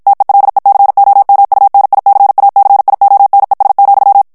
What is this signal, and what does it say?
Morse code, heard as a keyed tone, about 54 wpm, 800 Hz. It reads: TELECOMUNICACIONES,
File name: ESPAGNOL TEXTE EN CLAIR 2 VITESSE 50 MOTS MINUTES